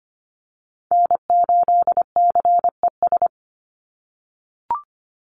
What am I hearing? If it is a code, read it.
Morse code, heard as a keyed tone, about 25 wpm, 700 Hz. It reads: N8CES
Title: USA Call Signs 25wpm